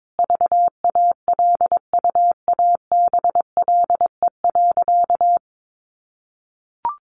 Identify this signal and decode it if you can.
Morse code, heard as a keyed tone, about 22 wpm, 700 Hz. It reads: VALUABLE.
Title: Sentences from Top 1100 Words 22wpm